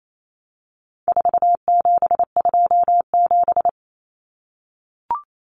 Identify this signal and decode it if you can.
Morse code, heard as a keyed tone, about 28 wpm, 700 Hz. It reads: 4727
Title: Numbers - 4-Digit 28wpm